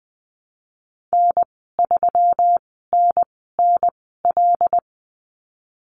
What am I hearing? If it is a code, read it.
Morse code, heard as a keyed tone, about 20 wpm, spaced as for 15 wpm, 700 Hz. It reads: N3NNL
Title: 140 Lesson 23g - Letter f - Call Signs 20wpm